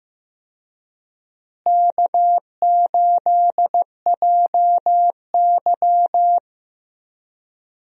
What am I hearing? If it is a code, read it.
Morse code, heard as a keyed tone, about 15 wpm, 700 Hz. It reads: K8JY
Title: USA Call Signs - 4 Characters - Repeated 2x - 15wpm